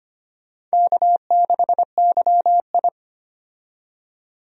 Morse code, heard as a keyed tone, about 25 wpm, 700 Hz. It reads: K6YI
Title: USA Call Signs - 4 Characters - 25wpm